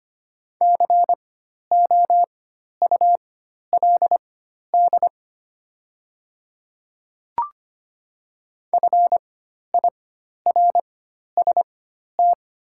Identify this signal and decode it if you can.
Morse code, heard as a keyed tone, about 25 wpm, spaced as for 12 wpm, 700 Hz. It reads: COULD FIRST
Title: 4x Character Spacing - Top 200 Words - 25wpm